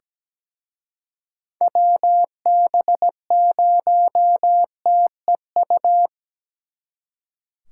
Morse code, heard as a keyed tone, about 20 wpm, 700 Hz. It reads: WB0TEU
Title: Speed Racing - USA Call Signs 17wpm